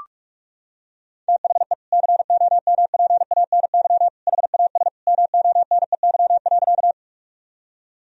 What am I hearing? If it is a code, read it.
Morse code, heard as a keyed tone, about 42 wpm, 700 Hz. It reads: THE COMPANY HAS MONEY.
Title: Sentences from Top 1000 Words - Review Part 3 - 45wpm